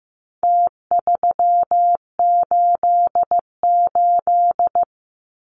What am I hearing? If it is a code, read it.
Morse code, heard as a keyed tone, about 15 wpm, 700 Hz. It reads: T388
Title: Numbers - Contest Style - 4-Digit 15wpm